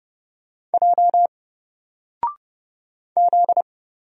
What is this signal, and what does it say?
Morse code, heard as a keyed tone, about 30 wpm, 700 Hz. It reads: J Z
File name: Single-Letter-Number-30wpm